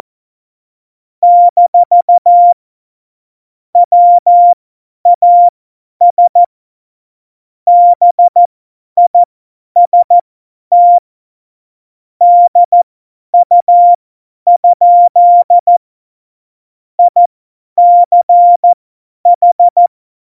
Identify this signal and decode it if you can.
Morse code, heard as a keyed tone, about 14 wpm, spaced as for 10 wpm, 700 Hz. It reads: - WAS BIST DU? ICH
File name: Morsekurs / Witze / Witz 15. Dezember 2020, 50 F70